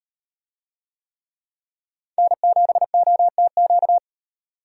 Morse code, heard as a keyed tone, about 38 wpm, 700 Hz. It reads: N7OTQ